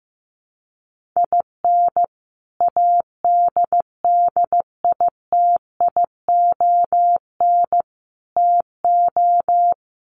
Morse code, heard as a keyed tone, about 15 wpm, 700 Hz. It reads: IN ADDITION TO
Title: Sets of 3 Words 15wpm